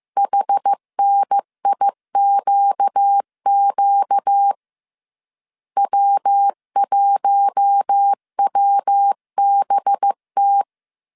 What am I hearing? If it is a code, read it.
Morse code, heard as a keyed tone, about 15 wpm, 800 Hz. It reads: HNIQQ W1WBT
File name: CW / DET / 1 6 1500C S C 14wpm 2951